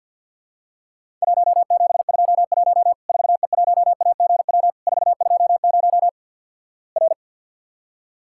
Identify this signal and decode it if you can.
Morse code, heard as a keyed tone, about 50 wpm, 700 Hz. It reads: 1721 4E1AGW 420 R
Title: Worked All Europe DX Contest - QTC Exchange - 50wpm